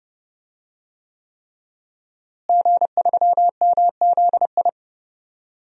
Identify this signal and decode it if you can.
Morse code, heard as a keyed tone, about 30 wpm, 700 Hz. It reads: G3MZI
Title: DXCC-100 Call Signs - 5 Characters - Repeated 3x - 30wpm